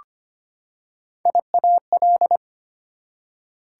Morse code, heard as a keyed tone, about 25 wpm, 700 Hz. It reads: IAL
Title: Prefixes and Suffixes 25wpm